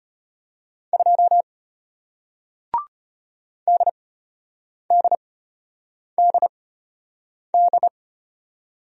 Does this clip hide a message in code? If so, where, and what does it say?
Yes: Morse code from 0.9 to 7.9 s, heard as a keyed tone, about 33 wpm, spaced as for 8 wpm, 700 Hz. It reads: 2 DDDD